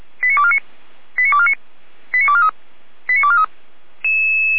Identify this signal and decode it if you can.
Five-tone selective call (ZVEI), heard as quick runs of five tones, tones of about 70 ms. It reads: 89249 89249 89243 89243